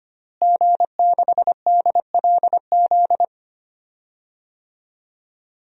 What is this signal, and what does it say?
Morse code, heard as a keyed tone, about 25 wpm, 700 Hz. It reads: G6DLZ
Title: DXCC-100 Call Signs - 5 Characters - Repeated 3x - 25wpm